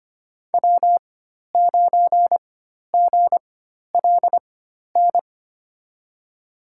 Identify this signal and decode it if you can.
Morse code, heard as a keyed tone, about 25 wpm, spaced as for 12 wpm, 700 Hz. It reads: W9GLN
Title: USA Call Signs - 5 Characters - 4x Character Spacing - 25wpm